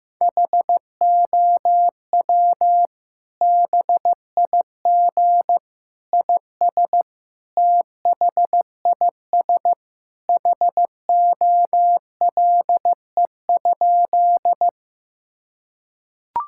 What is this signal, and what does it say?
Morse code, heard as a keyed tone, about 15 wpm, 700 Hz. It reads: HOW BIG IS THIS HOLE?
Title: Sentences from Top 1600 Words 15wpm